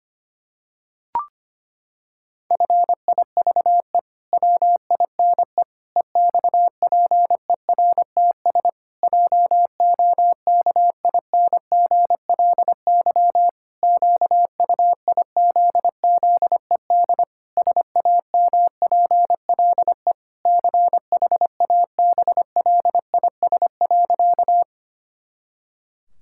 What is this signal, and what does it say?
Morse code, heard as a keyed tone, about 25 wpm, 700 Hz. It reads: FIVE WINE EXPERTS JOKINGLY QUIZZED SAMPLE CHABLIS.
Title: Pangram Challenge v2 - 25wpm